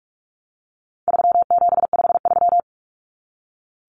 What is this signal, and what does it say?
Morse code, heard as a keyed tone, about 44 wpm, 700 Hz. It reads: 3753
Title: Common Cut Numbers - 4-Digit 45wpm